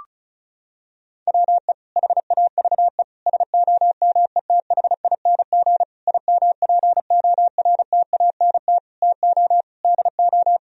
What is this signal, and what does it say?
Morse code, heard as a keyed tone, about 36 wpm, 700 Hz. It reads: WE HAVE SOMETHING IMPORTANT TO DO
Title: Sentences from Top 1000 Words - Review Part 4 - 35wpm